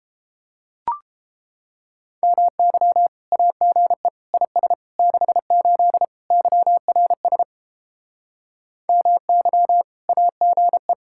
Morse code, heard as a keyed tone, about 31 wpm, 700 Hz. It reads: MY AGE IS 68 YRS MY AGE